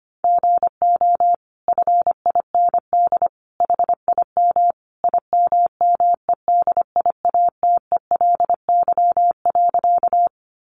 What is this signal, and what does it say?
Morse code, heard as a keyed tone, about 25 wpm, 700 Hz. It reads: GO FIND HIM IMMEDIATELY.
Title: Sentences from Top 1100 Words 25wpm